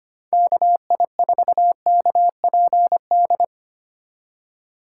Morse code, heard as a keyed tone, about 25 wpm, 700 Hz. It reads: KI4KPD
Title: USA Call Signs - 6 Characters - 4x Character Spacing - 25wpm